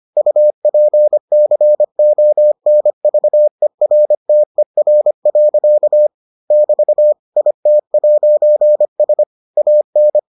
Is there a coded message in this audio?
Yes: Morse code from 0.2 to 10.2 s, heard as a keyed tone, about 25 wpm, 600 Hz. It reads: UPCONVERTER. <BT> IT'S AN